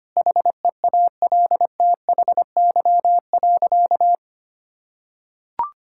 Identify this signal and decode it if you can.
Morse code, heard as a keyed tone, about 25 wpm, 700 Hz. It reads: HEALTHY.
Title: Sentences from Top 1000 Words - Review Part 5 - 25wpm